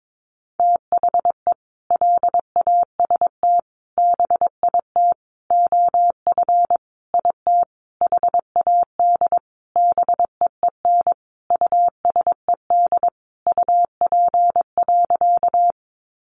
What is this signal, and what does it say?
Morse code, heard as a keyed tone, about 22 wpm, 700 Hz. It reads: THE LAST BIT OF IT HAD BEEN USED UP.
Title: Sentences from Top 900 Words 22wpm